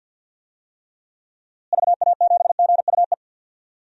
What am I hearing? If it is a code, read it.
Morse code, heard as a keyed tone, about 51 wpm, 700 Hz. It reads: VA7GUE
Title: DXCC-100 Call Signs - 6 Characters - Repeated 3x - 50wpm